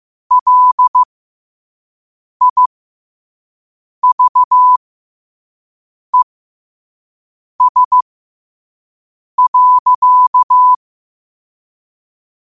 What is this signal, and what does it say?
Morse code, heard as a keyed tone, about 15 wpm, spaced as for 5 wpm, 1000 Hz. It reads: LIVES.
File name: qotdfiles / 5wpm